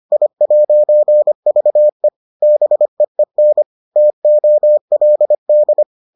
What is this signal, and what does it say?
Morse code, heard as a keyed tone, about 25 wpm, 600 Hz. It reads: I'VE BEEN TOLD